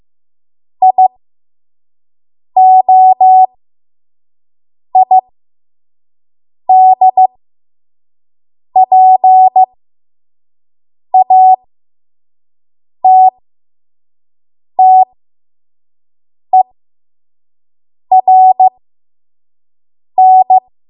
Morse code, heard as a keyed tone, about 15 wpm, spaced as for 5 wpm, 750 Hz. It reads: IOID PATTERN
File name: Morse / 05WPM